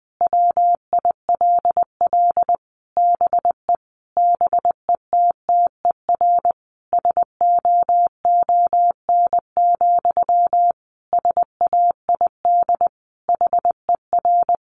Morse code, heard as a keyed tone, about 20 wpm, 700 Hz. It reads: WILL BE BETTER SOON, SAID HER
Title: Sentences from Top 1500 Words 20wpm